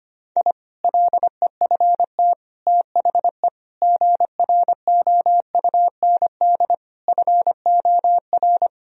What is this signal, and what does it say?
Morse code, heard as a keyed tone, about 25 wpm, 700 Hz. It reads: I LEFT THE GROUND FOR
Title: Sentences from Top 1000 Words - Review Part 2 - 25wpm